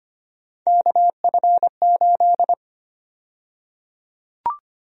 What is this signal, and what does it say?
Morse code, heard as a keyed tone, about 25 wpm, 700 Hz. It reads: KF8
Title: USA Call Sign Prefixes - 25wpm